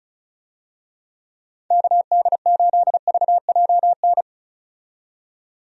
Morse code, heard as a keyed tone, about 34 wpm, 700 Hz. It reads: KD8VJN
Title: USA Call Signs - 6 Characters - 35wpm